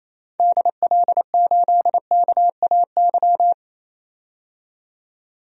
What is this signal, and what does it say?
Morse code, heard as a keyed tone, about 28 wpm, 700 Hz. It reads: DL8KAY
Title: DXCC-100 Call Signs - 6 Characters - 28wpm